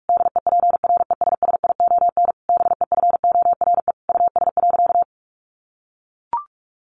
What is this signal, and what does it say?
Morse code, heard as a keyed tone, about 44 wpm, 700 Hz. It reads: DEPRESSION BEFORE US.
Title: Sentences from Top 1300 Words 45wpm